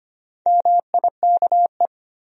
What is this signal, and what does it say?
Morse code, heard as a keyed tone, about 25 wpm, 700 Hz. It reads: MIKE
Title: USA Common Names - Max 5 Letters - 25wpm